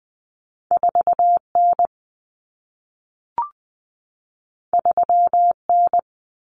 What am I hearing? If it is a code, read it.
Morse code, heard as a keyed tone, about 20 wpm, 700 Hz. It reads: 4N 3N
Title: Common Cut Numbers - 2-Digit 20wpm